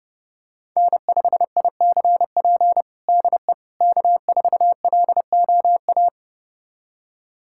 Morse code, heard as a keyed tone, about 32 wpm, 700 Hz. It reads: N5ICP DE K4LOA
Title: USA Callsign DE Callsign - 5 Characters - 30wpm